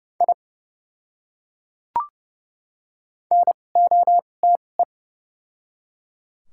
Morse code, heard as a keyed tone, about 32 wpm, spaced as for 22 wpm, 700 Hz. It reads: I NOTE
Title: 2x Character Spacing - Top 500 Words - 30wpm